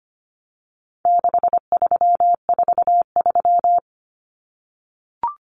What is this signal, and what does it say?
Morse code, heard as a keyed tone, about 25 wpm, 700 Hz. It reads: 6343
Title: Numbers - 4-Digit 25wpm